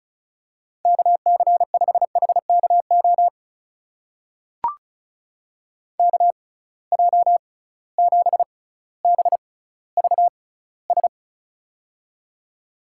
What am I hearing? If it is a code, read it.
Morse code, heard as a keyed tone, about 34 wpm, spaced as for 18 wpm, 700 Hz. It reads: KC5HKO KJ7BVS
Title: USA Call Signs - 6 Characters - 6x Character Spacing - 35wpm